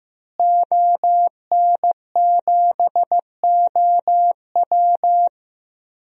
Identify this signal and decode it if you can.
Morse code, heard as a keyed tone, about 15 wpm, 700 Hz. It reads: ON7OW
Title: DXCC-100 Call Signs - 5 Characters - Repeated 3x - 15wpm